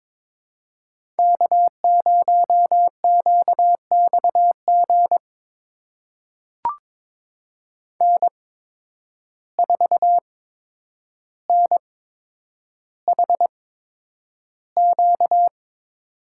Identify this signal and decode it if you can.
Morse code, heard as a keyed tone, about 21 wpm, spaced as for 9 wpm, 700 Hz. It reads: K0QXG N4NHQ
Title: USA Call Signs - 8x Character Spacing - 22wpm